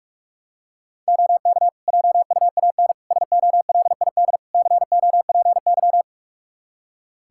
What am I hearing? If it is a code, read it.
Morse code, heard as a keyed tone, about 46 wpm, 700 Hz. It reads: OK JUAN SOLID COPY
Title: K1USN SST QSO Practice 45wpm